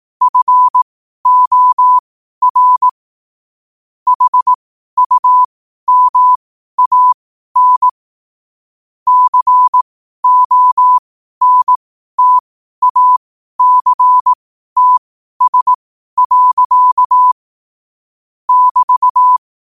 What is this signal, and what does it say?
Morse code, heard as a keyed tone, about 18 wpm, spaced as for 12 wpm, 1000 Hz. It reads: FOR HUMAN CONTACTS. <BT>
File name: qotdfiles / 12wpm